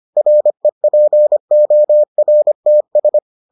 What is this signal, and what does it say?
Morse code, heard as a keyed tone, about 25 wpm, 600 Hz. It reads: REPORTS